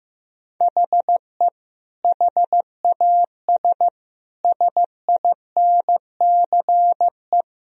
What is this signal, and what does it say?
Morse code, heard as a keyed tone, about 16 wpm, 700 Hz. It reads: HE HAS SINCE
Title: Sentences from Top 1000 Words - Review Part 2 - 15wpm